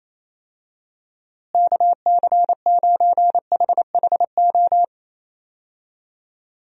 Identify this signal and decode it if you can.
Morse code, heard as a keyed tone, about 28 wpm, 700 Hz. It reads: KC9HHO
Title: USA Call Signs - 6 Characters - Repeated 2x - 28wpm